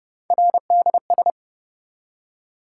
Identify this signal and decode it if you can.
Morse code, heard as a keyed tone, about 31 wpm, 700 Hz. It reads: RDS